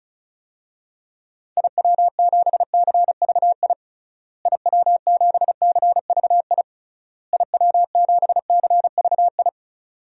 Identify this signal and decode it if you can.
Morse code, heard as a keyed tone, about 36 wpm, 700 Hz. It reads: IW7CVI IW7CVI IW7CVI
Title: DXCC-100 Call Signs - 6 Characters - Repeated 3x - 35wpm